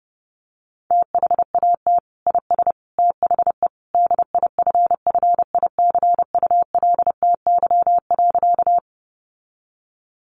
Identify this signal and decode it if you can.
Morse code, heard as a keyed tone, about 30 wpm, 700 Hz. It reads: THAT IS THE DIFFICULTY.
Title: Sentences from Top 1500 Words 30wpm